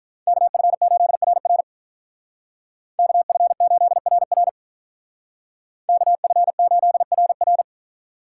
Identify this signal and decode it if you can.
Morse code, heard as a keyed tone, about 46 wpm, 700 Hz. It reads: KF8RR KF8RR KF8RR